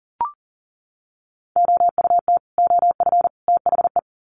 Morse code, heard as a keyed tone, about 40 wpm, 700 Hz. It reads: OUT OF THE
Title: Sets of 3 Words 40wpm